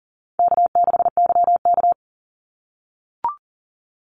Morse code, heard as a keyed tone, about 40 wpm, 700 Hz. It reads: K6YK
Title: NA SOTA Chaser Callsigns 40wpm